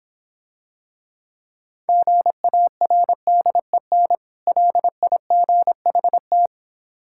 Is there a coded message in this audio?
Yes: Morse code from 1.9 to 6.5 s, heard as a keyed tone, about 26 wpm, 700 Hz. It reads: GARDEN LIGHT